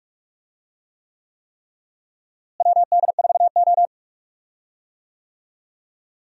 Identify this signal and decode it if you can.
Morse code, heard as a keyed tone, about 41 wpm, 700 Hz. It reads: WD4O
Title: USA Call Signs - 4 Characters - Repeated 3x - 45wpm